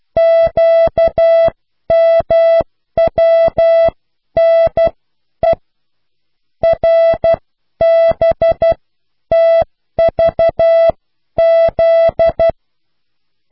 Morse code, heard as a keyed tone, about 12 wpm, spaced as for 10 wpm, 660 Hz. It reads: QMWNE RBTVZ